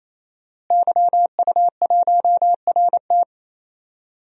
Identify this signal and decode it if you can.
Morse code, heard as a keyed tone, about 28 wpm, 700 Hz. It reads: YU1RT
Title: DXCC-100 Call Signs - 5 Characters - Repeated 3x - 28wpm